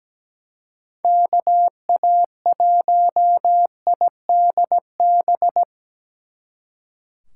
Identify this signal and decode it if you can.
Morse code, heard as a keyed tone, about 20 wpm, 700 Hz. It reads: KA1IDB
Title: Speed Racing - USA Call Signs 17wpm